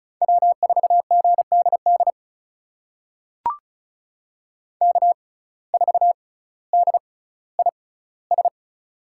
Morse code, heard as a keyed tone, about 36 wpm, spaced as for 18 wpm, 700 Hz. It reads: W4GDD K4DIS
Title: USA Call Signs - 5 Characters - 6x Character Spacing - 35wpm